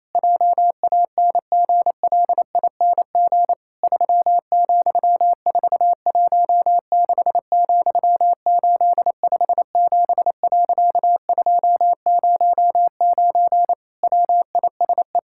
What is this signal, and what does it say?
Morse code, heard as a keyed tone, about 28 wpm, 700 Hz. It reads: JANGLING 3,416,857.209 WISE
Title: Pangram Challenge v2 - 28wpm